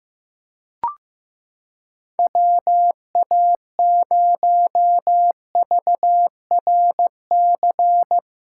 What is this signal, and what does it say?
Morse code, heard as a keyed tone, about 15 wpm, 700 Hz. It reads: WA0VRC
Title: USA Callsign DE Callsign - 6 Characters - 15wpm